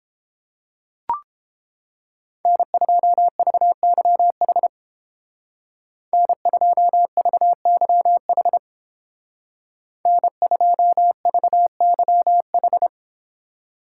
Morse code, heard as a keyed tone, about 31 wpm, 700 Hz. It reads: N2VYH N2VYH N2VYH